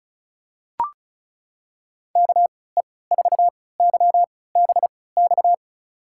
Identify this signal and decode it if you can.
Morse code, heard as a keyed tone, about 36 wpm, spaced as for 20 wpm, 700 Hz. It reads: KE4YBX
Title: USA Call Signs - 6 Characters - 3x Character Spacing - 35wpm